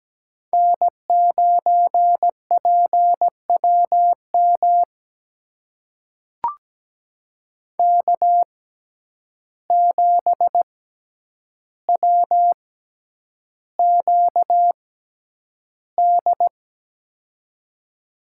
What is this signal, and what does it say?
Morse code, heard as a keyed tone, about 17 wpm, spaced as for 9 wpm, 700 Hz. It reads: N9PWM K7WQD
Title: USA Call Signs - 5 Characters - 6x Character Spacing - 17wpm